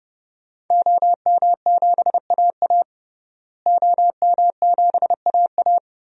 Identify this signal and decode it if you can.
Morse code, heard as a keyed tone, about 30 wpm, 700 Hz. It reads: OM7AA OM7AA
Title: DXCC-100 Call Signs - 5 Characters - Repeated 3x - 30wpm